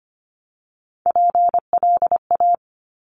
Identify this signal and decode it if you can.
Morse code, heard as a keyed tone, about 25 wpm, 700 Hz. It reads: PLA
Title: Top Three Letter Combinations in Words 25wpm